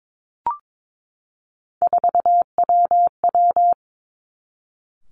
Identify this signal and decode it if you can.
Morse code, heard as a keyed tone, about 22 wpm, 700 Hz. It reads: 4WW
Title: Three Challenging Letter-Number Combinations 22wpm